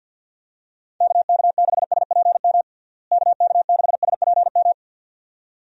Morse code, heard as a keyed tone, about 51 wpm, 700 Hz. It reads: KK6SPM KK6SPM
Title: USA Call Signs - 6 Characters - Repeated 2x - 50wpm